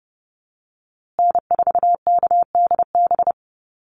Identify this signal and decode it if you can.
Morse code, heard as a keyed tone, about 30 wpm, 700 Hz. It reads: N4KDB
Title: USA Call Signs Repeated 2x 30wpm